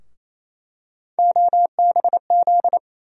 Morse code, heard as a keyed tone, about 28 wpm, 700 Hz. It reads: OBZ
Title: USA Call Sign Postfixes - 3 Letters 28wpm